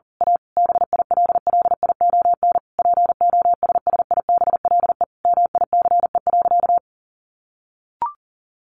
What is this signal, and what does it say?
Morse code, heard as a keyed tone, about 40 wpm, 700 Hz. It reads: A BILLION POSSIBLE MICE.
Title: A Sound Of Thunder 40wpm